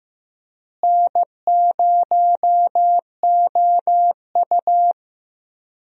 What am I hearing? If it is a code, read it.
Morse code, heard as a keyed tone, about 15 wpm, 700 Hz. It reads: N0OU
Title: USA Call Signs - 4 Characters - Repeated 3x - 15wpm